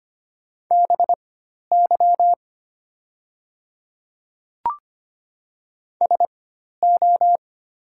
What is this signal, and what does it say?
Morse code, heard as a keyed tone, about 25 wpm, spaced as for 12 wpm, 700 Hz. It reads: BY SO
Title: 4x Character Spacing - Top 100 Words - 25wpm